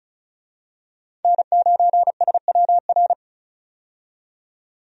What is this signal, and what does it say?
Morse code, heard as a keyed tone, about 34 wpm, 700 Hz. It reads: N9SWR
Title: USA Call Signs - 5 Characters - 35wpm